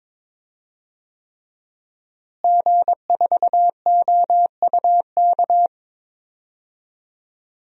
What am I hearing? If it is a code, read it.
Morse code, heard as a keyed tone, about 22 wpm, 700 Hz. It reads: G4OUK
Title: DXCC-100 Call Signs - 5 Characters - Repeated 3x - 22wpm